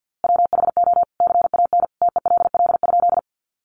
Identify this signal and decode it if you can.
Morse code, heard as a keyed tone, about 50 wpm, 700 Hz. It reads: WHO CAN TELL?
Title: A Sound Of Thunder 50wpm